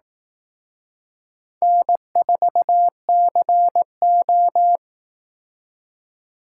Morse code, heard as a keyed tone, about 18 wpm, 700 Hz. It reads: N4CO
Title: Speed Racing - USA Call Signs 15wpm